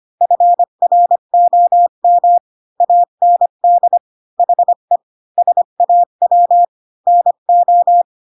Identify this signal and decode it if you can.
Morse code, heard as a keyed tone, about 25 wpm, 700 Hz. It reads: FROM AND HE SAW NO
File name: Pinocchio / Pinocchio ch1 20 25 ik1hge